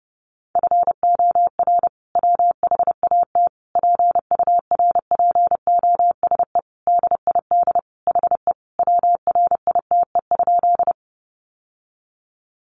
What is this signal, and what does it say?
Morse code, heard as a keyed tone, about 30 wpm, 700 Hz. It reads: FOR WHAT PURPOSE DID HE WRITE?
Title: Sentences from Top 800 Words 30wpm